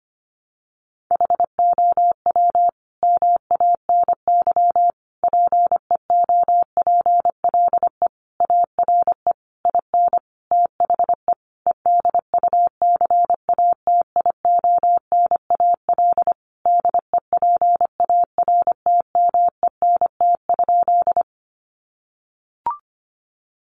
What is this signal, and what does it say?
Morse code, heard as a keyed tone, about 25 wpm, 700 Hz. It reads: HOW MANY PEOPLE ARE IN THE EDUCATIONAL DEPARTMENT?
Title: Sentences from Top 1200 Words 25wpm